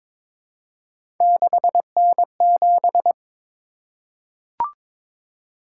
Morse code, heard as a keyed tone, about 22 wpm, 700 Hz. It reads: 6N7
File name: practice / 22wpm / Three Letter-Number Combinations - 22wpm